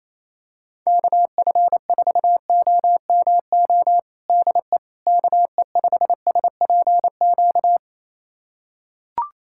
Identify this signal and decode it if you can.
Morse code, heard as a keyed tone, about 28 wpm, 700 Hz. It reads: KF4OMO DE KE5SPQ
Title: USA Callsign DE Callsign - 6 Characters - 28wpm